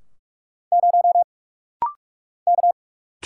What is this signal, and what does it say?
Morse code, heard as a keyed tone, about 44 wpm, 700 Hz. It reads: 0 K
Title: Single Letter-Number - Rapid-Fire 45wpm